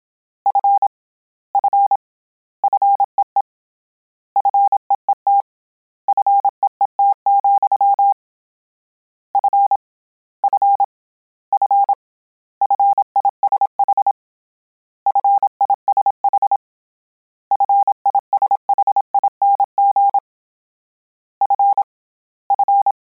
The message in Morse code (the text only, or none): F F FEE FEET FEET, F F F FISH FISH FISHING F F